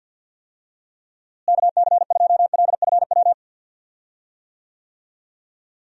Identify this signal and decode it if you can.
Morse code, heard as a keyed tone, about 46 wpm, 700 Hz. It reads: KC2LFW
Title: USA Call Signs - 6 Characters - Repeated 2x - 50wpm